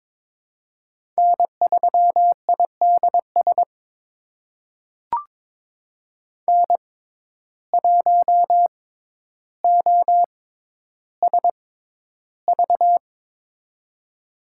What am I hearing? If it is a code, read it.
Morse code, heard as a keyed tone, about 22 wpm, spaced as for 11 wpm, 700 Hz. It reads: N3IDS N1OSV
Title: USA Call Signs - 5 Characters - 6x Character Spacing - 22wpm